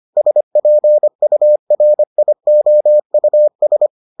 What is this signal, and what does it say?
Morse code, heard as a keyed tone, about 25 wpm, 600 Hz. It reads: SPURIOUS